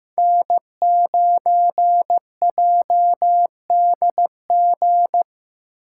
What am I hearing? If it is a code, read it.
Morse code, heard as a keyed tone, about 15 wpm, 700 Hz. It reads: N9JDG
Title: USA Call Signs - 5 Characters - 8x Character Spacing - 15wpm